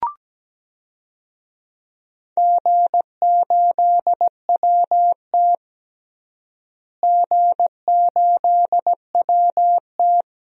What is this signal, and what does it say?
Morse code, heard as a keyed tone, about 17 wpm, 700 Hz. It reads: G8WT G8WT
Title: DXCC-100 Call Signs - 4 Characters - Repeated 2x - 17wpm